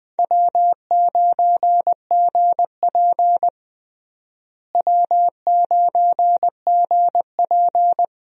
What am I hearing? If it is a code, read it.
Morse code, heard as a keyed tone, about 20 wpm, 700 Hz. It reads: W9GP W9GP
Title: USA Call Signs - 4 Characters - Repeated 2x - 20wpm